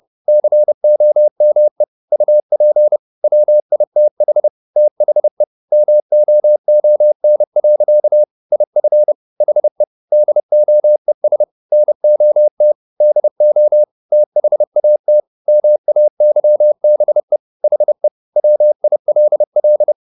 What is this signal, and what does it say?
Morse code, heard as a keyed tone, about 30 wpm, 600 Hz. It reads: COME UP WITH THE MOON. IF HE DOES NOT DO THAT MAYBE HE WILL
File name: the old man and the sea / the old man and the sea 1 30wpm